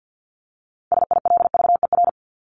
Morse code, heard as a keyed tone, about 50 wpm, 700 Hz. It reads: SILVER
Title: Top 1751-2000 Words 50wpm